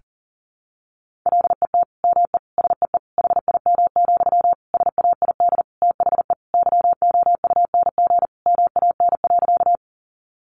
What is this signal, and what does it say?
Morse code, heard as a keyed tone, about 40 wpm, 700 Hz. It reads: LET ME SEE HIM, SAID THE YOUNG MAN.
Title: Sentences from Top 400 Words 40wpm